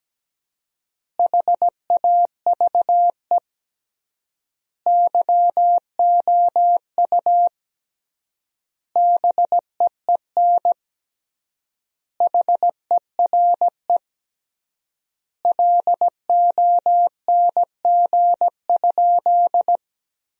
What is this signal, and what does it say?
Morse code, heard as a keyed tone, about 17 wpm, 700 Hz. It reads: HAVE YOU BEEN HERE LONG?
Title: 3x Word-Spacing - Sentences from Top 200 Words - 17wpm